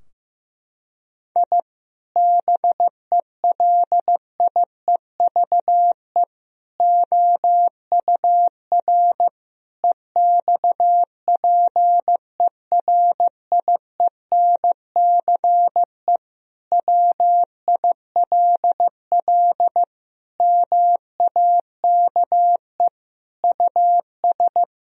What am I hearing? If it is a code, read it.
Morse code, heard as a keyed tone, about 15 wpm, 700 Hz. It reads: I BELIEVE OUR EXPERIENCE WILL MAKE US
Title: Sentences from Top 1000 Words - Review Part 3 - 15wpm